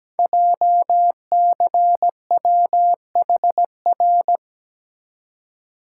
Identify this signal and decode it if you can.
Morse code, heard as a keyed tone, about 17 wpm, 700 Hz. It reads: JCWHR
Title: Code Groups - 5 Letters - 17wpm